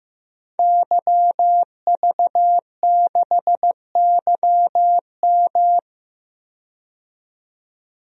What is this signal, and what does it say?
Morse code, heard as a keyed tone, about 15 wpm, 700 Hz. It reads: YV6YM
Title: DXCC-100 Call Signs - 5 Characters - 15wpm